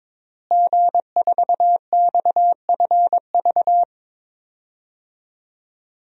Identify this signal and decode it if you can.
Morse code, heard as a keyed tone, about 22 wpm, 700 Hz. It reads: G4XFV
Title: DXCC-100 Call Signs - 5 Characters - Repeated 3x - 22wpm